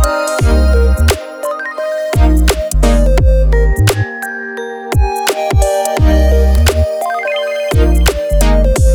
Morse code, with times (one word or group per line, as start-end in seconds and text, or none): none